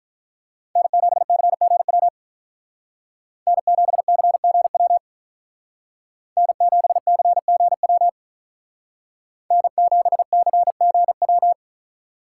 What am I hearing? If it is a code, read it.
Morse code, heard as a keyed tone, about 45 wpm, 700 Hz. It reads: N7CGW N7CGW N7CGW N7CGW